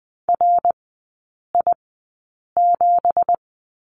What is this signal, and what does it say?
Morse code, heard as a keyed tone, about 20 wpm, spaced as for 8 wpm, 700 Hz. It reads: RI7
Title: 184 Lesson 30e - Letter b - Three Characters 20wpm